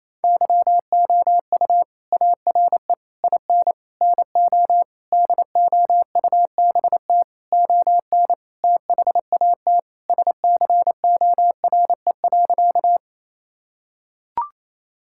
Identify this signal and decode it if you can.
Morse code, heard as a keyed tone, about 28 wpm, 700 Hz. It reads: YOU ARE IN NO DOUBT ON THAT SCORE.
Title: Sentences from Top 1600 Words 28wpm